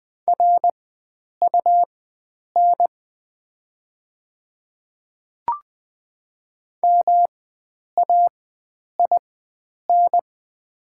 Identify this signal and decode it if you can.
Morse code, heard as a keyed tone, about 20 wpm, spaced as for 9 wpm, 700 Hz. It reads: RUN MAIN
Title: 4x Character Spacing - Top 300 Words - 20wpm